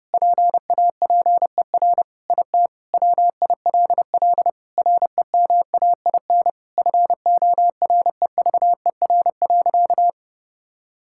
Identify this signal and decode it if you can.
Morse code, heard as a keyed tone, about 30 wpm, 700 Hz. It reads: PAPER IT WILL REMAIN FOREVER.
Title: Sentences from Top 1600 Words 30wpm